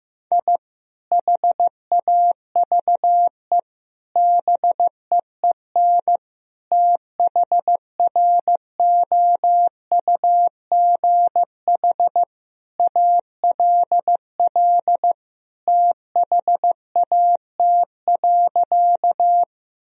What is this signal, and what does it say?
Morse code, heard as a keyed tone, about 16 wpm, 700 Hz. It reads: I HAVE BEEN THROUGH ALL THAT.
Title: Sentences from Top 1000 Words - Review Part 1 - 15wpm